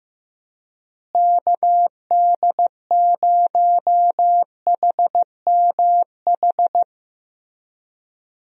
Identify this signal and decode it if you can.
Morse code, heard as a keyed tone, about 16 wpm, 700 Hz. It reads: KD0HMH
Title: USA Call Signs - 6 Characters - Repeated 2x - 15wpm